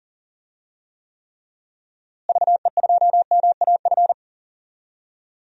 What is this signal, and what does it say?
Morse code, heard as a keyed tone, about 37 wpm, 700 Hz. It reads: VE2MAF
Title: DXCC-100 Call Signs - 6 Characters - Repeated 3x - 40wpm